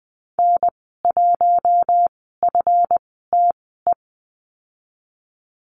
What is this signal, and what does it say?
Morse code, heard as a keyed tone, about 20 wpm, spaced as for 14 wpm, 700 Hz. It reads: N1FTE
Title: 140 Lesson 23g - Letter f - Call Signs 20wpm